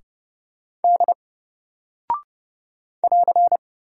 Morse code, heard as a keyed tone, about 31 wpm, 700 Hz. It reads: D <AR>
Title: Single Character (Letter:Number:Punctuation:Prosign) 30wpm